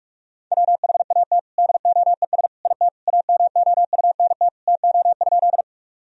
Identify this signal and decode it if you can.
Morse code, heard as a keyed tone, about 46 wpm, 700 Hz. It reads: WHAT DOES IT AMOUNT TO?
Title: Sentences from Top 1000 Words - Review Part 4 - 45wpm